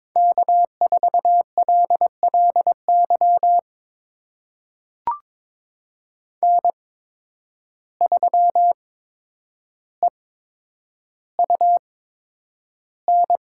Morse code, heard as a keyed tone, about 22 wpm, spaced as for 9 wpm, 700 Hz. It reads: K4LLY N3EUN
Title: USA Call Signs - 5 Characters - 8x Character Spacing - 22wpm